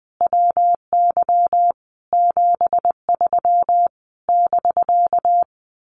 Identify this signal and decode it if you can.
Morse code, heard as a keyed tone, about 20 wpm, 700 Hz. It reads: WY 73 <BK>
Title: Common QSO Phrases Encore 20wpm